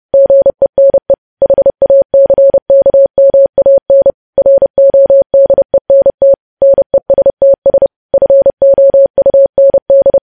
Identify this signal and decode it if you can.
Morse code, heard as a keyed tone, about 30 wpm, 550 Hz. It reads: GENE HACKMAN RODENT NESTS FOUND
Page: news headlines in morse code at 30 wpm